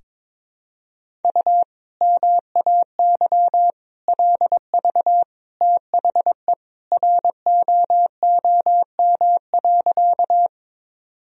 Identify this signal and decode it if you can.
Morse code, heard as a keyed tone, about 22 wpm, 700 Hz. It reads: U MAY LV THE ROOM.
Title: Sentences from Top 400 Words and Abbreviations 22wpm